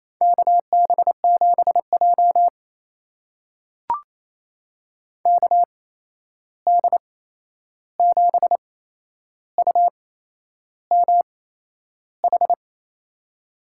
Morse code, heard as a keyed tone, about 28 wpm, spaced as for 11 wpm, 700 Hz. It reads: KB7J KD7UMH
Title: USA Call Signs - 8x Character Spacing - 28wpm